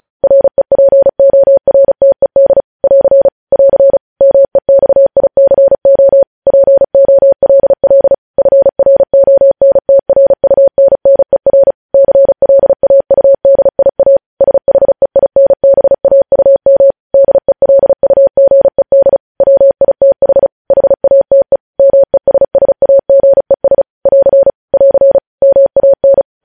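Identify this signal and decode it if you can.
Morse code, heard as a keyed tone, about 36 wpm, 550 Hz. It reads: REPORTED <AR> <AR> MEXICO POLL FRONTRUNNER CLAUDIA SHEINBAUM DELUGED WITH HATE MESSAGES <AR> <AR> MAN
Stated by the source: Morse code news headlines practice sent at 35 wpm.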